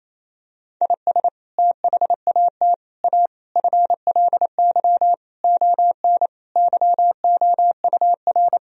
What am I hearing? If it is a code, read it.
Morse code, heard as a keyed tone, about 28 wpm, 700 Hz. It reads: IS THAT A FLY ON YOUR
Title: Sentences from Top 1000 Words - Review Part 2 - 28wpm